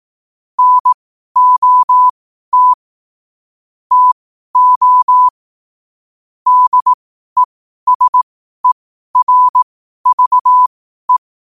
Morse code, heard as a keyed tone, about 18 wpm, spaced as for 12 wpm, 1000 Hz. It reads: NOT TO DESERVE
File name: qotdfiles / 12wpm